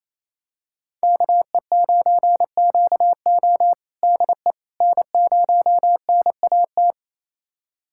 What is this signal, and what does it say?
Morse code, heard as a keyed tone, about 28 wpm, 700 Hz. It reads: KE9QO DE N0NAT
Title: USA Callsign DE Callsign - 5 Characters - 28wpm